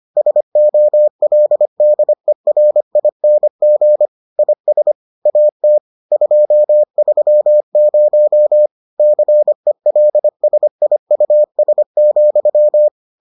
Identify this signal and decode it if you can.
Morse code, heard as a keyed tone, about 25 wpm, 600 Hz. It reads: SOLDERING IS AT 230 CELSIUS,